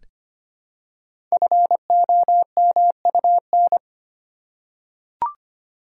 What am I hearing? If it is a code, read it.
Morse code, heard as a keyed tone, about 25 wpm, 700 Hz. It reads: FOMUN
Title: Code Groups - 5 Letters - 25wpm